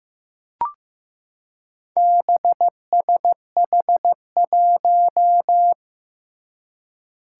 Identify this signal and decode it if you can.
Morse code, heard as a keyed tone, about 15 wpm, 700 Hz. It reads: BSH1
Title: Four Challenging Letter-Number Combinations 15wpm